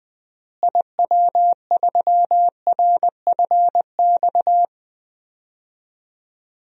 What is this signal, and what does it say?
Morse code, heard as a keyed tone, about 20 wpm, 700 Hz. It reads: IW3RFX
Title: DXCC-100 Call Signs - 6 Characters - Repeated 2x - 20wpm